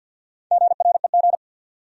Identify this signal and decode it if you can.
Morse code, heard as a keyed tone, about 46 wpm, 700 Hz. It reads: GREG
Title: USA Common Names - Max 4 Letters - 50wpm